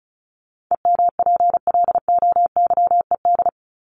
Morse code, heard as a keyed tone, about 34 wpm, 700 Hz. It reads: EMPLOYED
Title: ED Suffix - Top 100 Words with Base Words 35wpm